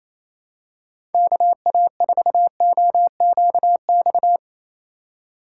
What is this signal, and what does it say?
Morse code, heard as a keyed tone, about 28 wpm, 700 Hz. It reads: KA4OQX
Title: USA Call Signs - 6 Characters - 3x Character Spacing - 28wpm